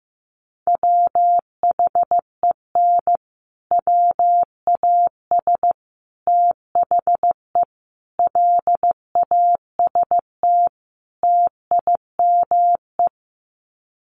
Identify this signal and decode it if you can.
Morse code, heard as a keyed tone, about 15 wpm, 700 Hz. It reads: WHEN WAS THE LAST TIME
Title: Sets of 5 Words Encore 15wpm